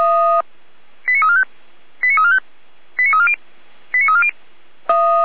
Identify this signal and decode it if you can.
Five-tone selective call (ZVEI), heard as quick runs of five tones, tones of about 70 ms. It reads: 89345 89345 89340 89340
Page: /455/SDR/ZVEI/89501/